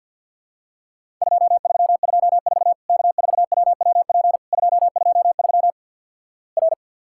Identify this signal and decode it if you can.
Morse code, heard as a keyed tone, about 50 wpm, 700 Hz. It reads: 2324 K4WWP 223 R
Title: Worked All Europe DX Contest - QTC Exchange - 50wpm